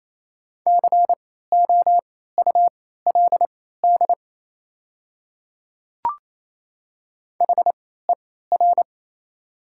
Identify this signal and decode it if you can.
Morse code, heard as a keyed tone, about 28 wpm, spaced as for 16 wpm, 700 Hz. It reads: COULD HER
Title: 3x Character Spacing - Top 100 Words - 28wpm